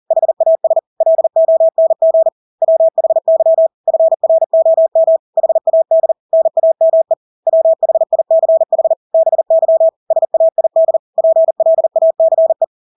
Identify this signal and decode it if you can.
Morse code, heard as a keyed tone, about 40 wpm, 650 Hz. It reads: HAS LONG WHY FROM HAD NAME WHICH BY SAID PLACE
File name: Engelske ord 40-40 wpm